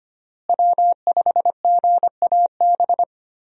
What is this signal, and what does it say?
Morse code, heard as a keyed tone, about 25 wpm, 700 Hz. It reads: W5GAB
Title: USA Call Signs - 5 Characters - Repeated 2x - 25wpm